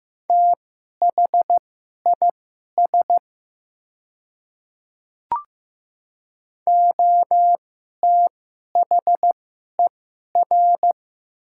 Morse code, heard as a keyed tone, about 15 wpm, spaced as for 11 wpm, 700 Hz. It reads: THIS OTHER
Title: 2x Character Spacing - Top 100 Words - 15wpm